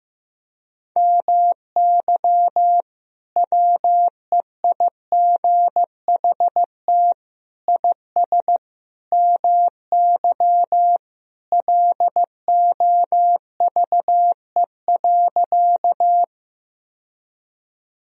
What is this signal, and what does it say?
Morse code, heard as a keyed tone, about 15 wpm, 700 Hz. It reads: MY WEIGHT IS MY LOVE.
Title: Sentences from Top 1000 Words - Review Part 3 - 15wpm